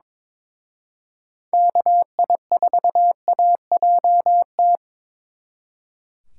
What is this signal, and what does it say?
Morse code, heard as a keyed tone, about 26 wpm, 700 Hz. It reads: KI4AJT
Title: Speed Racing - USA Call Signs 22wpm